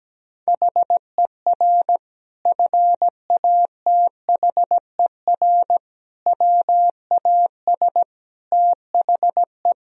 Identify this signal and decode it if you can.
Morse code, heard as a keyed tone, about 17 wpm, 700 Hz. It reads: HER FATHER WAS THE
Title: Sentences from Top 1000 Words - Review Part 4 - 17wpm